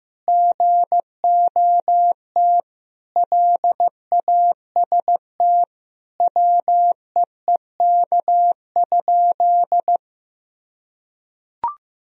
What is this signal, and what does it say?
Morse code, heard as a keyed tone, about 15 wpm, 700 Hz. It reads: GOT LAST WEEK?
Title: Sentences from Top 1000 Words - Review Part 3 - 15wpm